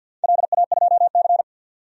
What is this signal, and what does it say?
Morse code, heard as a keyed tone, about 52 wpm, 700 Hz. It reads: LA2C
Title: DXCC-100 Call Signs - 4 Characters - 50wpm